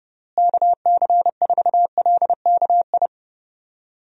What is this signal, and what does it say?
Morse code, heard as a keyed tone, about 30 wpm, 700 Hz. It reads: KC4LKI
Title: USA Call Signs - 6 Characters - Repeated 3x - 30wpm